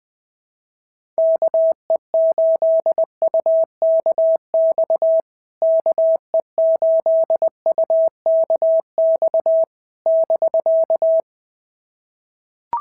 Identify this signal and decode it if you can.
Morse code, heard as a keyed tone, about 20 wpm, 650 Hz. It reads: KE8UKX KE8UKX <BK>
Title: POTA QSO Practice - 20wpm